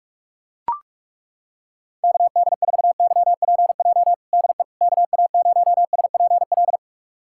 Morse code, heard as a keyed tone, about 42 wpm, 700 Hz. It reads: KD4YPJ DE KA0SPL
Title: USA Callsign DE Callsign - 6 Characters - 45wpm